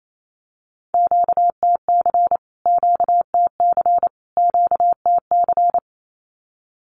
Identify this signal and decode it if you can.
Morse code, heard as a keyed tone, about 28 wpm, 700 Hz. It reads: QTC QTC QTC
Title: Q-Codes Repeated 3x 28wpm